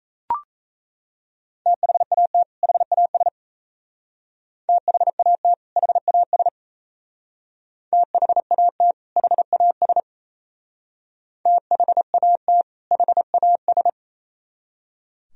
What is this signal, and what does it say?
Morse code, heard as a keyed tone, about 35 wpm, 700 Hz. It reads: THAT HAS THAT HAS THAT HAS THAT HAS
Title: Speed-Racing - Sets of 2 Words 28wpm